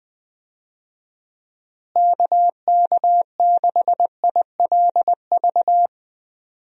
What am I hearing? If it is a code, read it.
Morse code, heard as a keyed tone, about 20 wpm, 700 Hz. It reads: KK6ILV
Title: USA Call Signs - 6 Characters - Repeated 2x - 20wpm